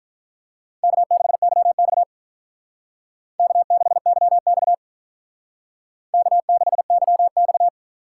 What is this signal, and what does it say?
Morse code, heard as a keyed tone, about 43 wpm, 700 Hz. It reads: K6YX K6YX K6YX
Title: Speed Racing - USA Call Signs 35wpm